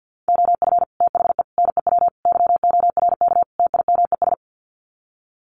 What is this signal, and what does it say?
Morse code, heard as a keyed tone, about 50 wpm, 700 Hz. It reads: OF THE NEW YORK TIMES
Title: Sets of 5 Words 50wpm